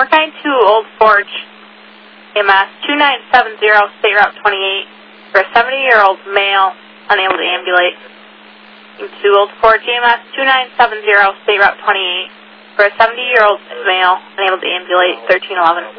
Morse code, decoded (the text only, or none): none